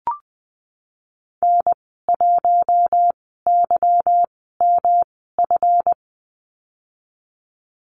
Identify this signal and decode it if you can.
Morse code, heard as a keyed tone, about 20 wpm, spaced as for 15 wpm, 700 Hz. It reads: N1YMF
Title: 147 Lesson 24g - Letter y - Call Signs 20wpm